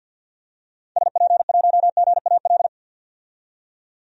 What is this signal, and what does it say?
Morse code, heard as a keyed tone, about 47 wpm, 700 Hz. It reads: SP1GAL
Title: DXCC-100 Call Signs - 6 Characters - 50wpm